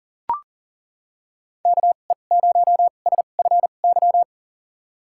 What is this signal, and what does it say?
Morse code, heard as a keyed tone, about 39 wpm, spaced as for 29 wpm, 700 Hz. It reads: KE0SFY
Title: USA Call Signs - 6 Characters - 2x Character Spacing - 40wpm